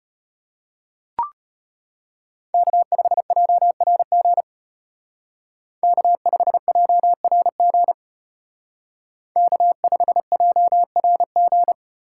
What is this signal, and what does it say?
Morse code, heard as a keyed tone, about 32 wpm, 700 Hz. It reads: K5JRG K5JRG K5JRG